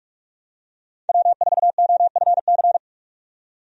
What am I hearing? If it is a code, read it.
Morse code, heard as a keyed tone, about 47 wpm, 700 Hz. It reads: W4OFC
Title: USA Call Signs - 5 Characters - 45wpm